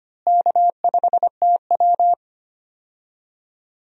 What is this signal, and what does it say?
Morse code, heard as a keyed tone, about 25 wpm, 700 Hz. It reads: K5TW
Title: USA Call Signs - 4 Characters - Repeated 3x - 25wpm